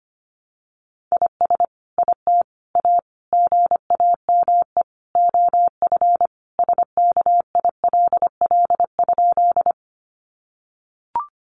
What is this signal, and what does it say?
Morse code, heard as a keyed tone, about 25 wpm, 700 Hz. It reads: IS IT A GAME OF SKILL?
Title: Sentences from Top 1200 Words 25wpm